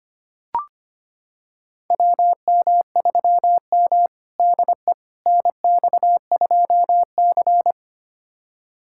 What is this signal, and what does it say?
Morse code, heard as a keyed tone, about 25 wpm, 700 Hz. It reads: WM3M DE NX2C
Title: USA Callsign DE Callsign - 4 Characters - 25wpm